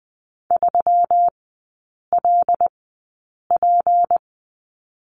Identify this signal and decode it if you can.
Morse code, heard as a keyed tone, about 20 wpm, spaced as for 8 wpm, 700 Hz. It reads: 3LP
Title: 152 Lesson 25e - Letter p - Three Characters 20wpm